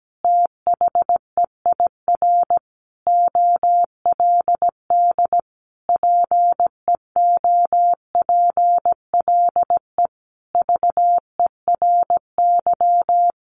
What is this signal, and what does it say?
Morse code, heard as a keyed tone, about 17 wpm, 700 Hz. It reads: THEIR OLD PEOPLE VERY
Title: Sentences from Top 1300 Words 17wpm